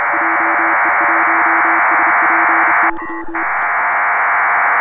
Morse code, heard as a keyed tone, about 31 wpm, 330 Hz, with noise. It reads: JE1SPY
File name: Pile-up with PACTOR3